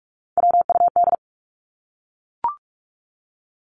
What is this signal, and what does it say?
Morse code, heard as a keyed tone, about 44 wpm, 700 Hz. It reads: WUD
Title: Abbreviations Repeated 3x - 45wpm